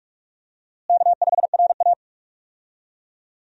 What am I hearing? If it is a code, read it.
Morse code, heard as a keyed tone, about 46 wpm, 700 Hz. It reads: K5RA
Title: USA Call Signs - 4 Characters - Repeated 2x - 45wpm